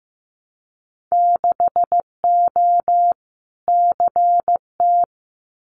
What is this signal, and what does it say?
Morse code, heard as a keyed tone, about 15 wpm, 700 Hz. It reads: 6O CT
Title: Winter Field Day Exchanges Encore 15wpm